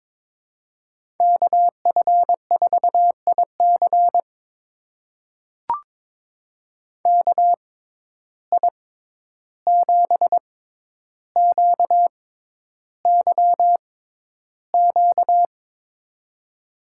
Morse code, heard as a keyed tone, about 22 wpm, spaced as for 11 wpm, 700 Hz. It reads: KF4IC KI7QYQ
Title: USA Call Signs - 6x Character Spacing - 22wpm